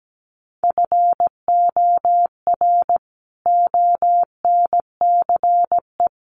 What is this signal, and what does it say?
Morse code, heard as a keyed tone, about 17 wpm, 700 Hz. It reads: FOR ONCE
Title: Sentences from Top 300 Words 17wpm